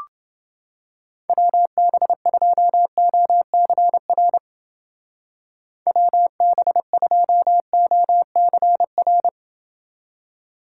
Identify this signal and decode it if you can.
Morse code, heard as a keyed tone, about 29 wpm, 700 Hz. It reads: WB2OCR WB2OCR